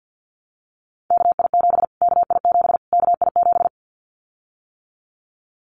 Morse code, heard as a keyed tone, about 50 wpm, 700 Hz. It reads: KI7 KI7 KI7
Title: Callsign Prefixes for Top 25 DXCC - Repeated 3x - 50wpm